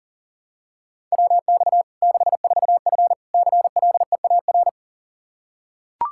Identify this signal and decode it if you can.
Morse code, heard as a keyed tone, about 40 wpm, 700 Hz. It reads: WX 64F CLEAR
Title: K1USN SST QSO Practice 40wpm